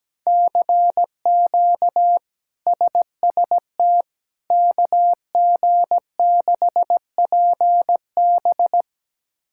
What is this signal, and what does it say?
Morse code, heard as a keyed tone, about 17 wpm, 700 Hz. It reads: CQ SST KG6PB
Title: K1USN SST QSO Practice - 17wpm